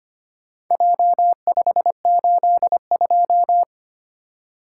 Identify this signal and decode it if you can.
Morse code, heard as a keyed tone, about 25 wpm, 700 Hz. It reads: J582
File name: practice / 25wpm / Four Letter-Number Combinations - 25wpm